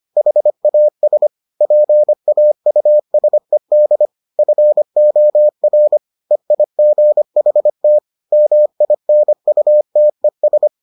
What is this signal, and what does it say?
Morse code, heard as a keyed tone, about 25 wpm, 600 Hz. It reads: HAS PAUSED FOR EIGHT MINUTES